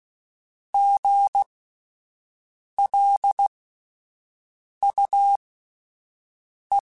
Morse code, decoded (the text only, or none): GLUE